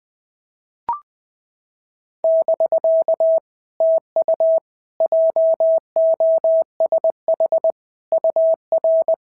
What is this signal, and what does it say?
Morse code, heard as a keyed tone, about 19 wpm, 650 Hz. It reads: <BK> TU JOSH UR
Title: POTA QSO Practice - 20wpm